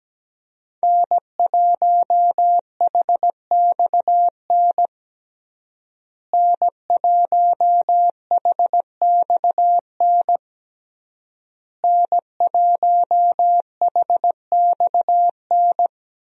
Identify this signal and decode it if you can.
Morse code, heard as a keyed tone, about 17 wpm, 700 Hz. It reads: N1HXN N1HXN N1HXN
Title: USA Call Signs - 5 Characters - Repeated 3x - 17wpm